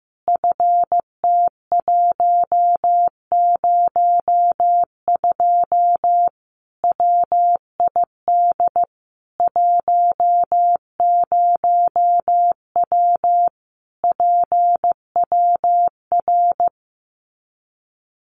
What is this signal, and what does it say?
Morse code, heard as a keyed tone, about 15 wpm, 700 Hz. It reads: FT102 WID 10W PWR
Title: Common QSO Phrases 15wpm